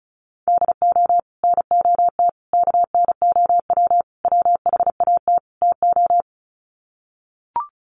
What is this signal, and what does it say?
Morse code, heard as a keyed tone, about 34 wpm, 700 Hz. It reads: DO NOT KNOW WHAT TO
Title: Sets of 5 Words 35wpm